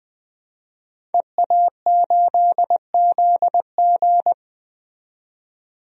Morse code, heard as a keyed tone, about 20 wpm, 700 Hz. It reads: EA8ZG
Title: DXCC-100 Call Signs - 5 Characters - 20wpm